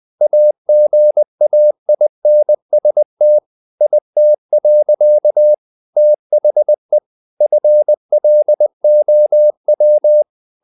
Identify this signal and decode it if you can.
Morse code, heard as a keyed tone, about 20 wpm, 600 Hz. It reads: AGAINST IT. THE FLOW
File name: the old man and the sea part 2 20wpm